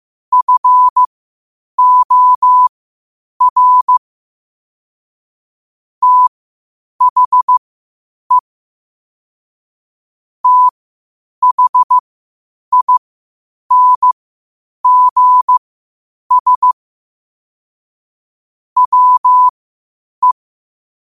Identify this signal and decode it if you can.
Morse code, heard as a keyed tone, about 15 wpm, spaced as for 8 wpm, 1000 Hz. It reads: FOR THE THINGS WE
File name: qotdfiles / 8wpm